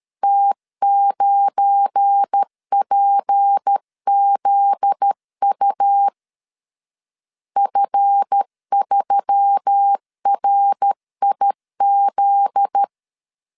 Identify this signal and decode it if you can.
Morse code, heard as a keyed tone, about 13 wpm, 800 Hz. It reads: T9PZU F3RIZ
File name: LEZIONI / 7 GF9Z / 7 GF9Z 12wpm 2147